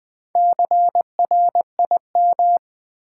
Morse code, heard as a keyed tone, about 20 wpm, 700 Hz. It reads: CRIM
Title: Build A Word - 2 Letters At A Time - 8-Letter Words - 20wpm